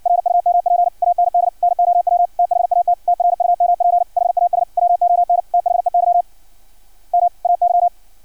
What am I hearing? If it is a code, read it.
Morse code, heard as a keyed tone, about 56 wpm, 700 Hz. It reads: RUMP AND NOW THAT TRUMP HAS WON THEY MAY
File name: archived / CRSnet